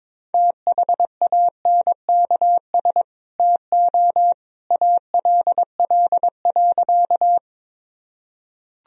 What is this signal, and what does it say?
Morse code, heard as a keyed tone, about 22 wpm, 700 Hz. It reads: THANKS TO ALL.
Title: Sentences from Top 1000 Words - Review Part 3 - 22wpm